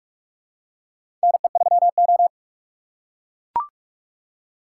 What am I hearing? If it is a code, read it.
Morse code, heard as a keyed tone, about 44 wpm, 700 Hz. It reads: NE3O
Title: USA Call Signs - 4 Characters - 2x Character Spacing - 45wpm